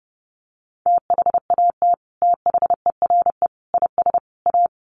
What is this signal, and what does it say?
Morse code, heard as a keyed tone, about 30 wpm, 700 Hz. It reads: THAT THERE IS A
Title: Sets of 4 Words 30wpm